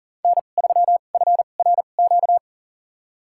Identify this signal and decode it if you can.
Morse code, heard as a keyed tone, about 40 wpm, spaced as for 29 wpm, 700 Hz. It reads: N3FRQ
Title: USA Call Signs - 5 Characters - 2x Character Spacing - 40wpm